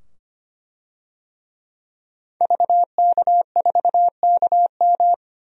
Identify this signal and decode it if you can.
Morse code, heard as a keyed tone, about 25 wpm, 700 Hz. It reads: VK4KM
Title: DXCC-100 Call Signs - 5 Characters - 25wpm